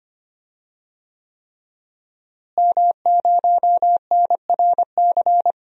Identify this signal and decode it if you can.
Morse code, heard as a keyed tone, about 25 wpm, 700 Hz. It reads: M0NRC
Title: DXCC-100 Call Signs - 5 Characters - Repeated 3x - 25wpm